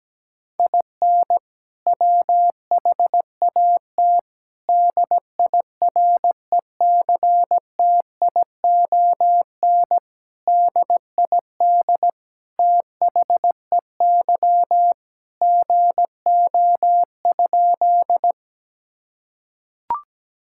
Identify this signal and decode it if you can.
Morse code, heard as a keyed tone, about 17 wpm, 700 Hz. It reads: IN WHAT DIRECTION DID THEY GO?
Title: Sentences from Top 1000 Words - Review Part 5 - 17wpm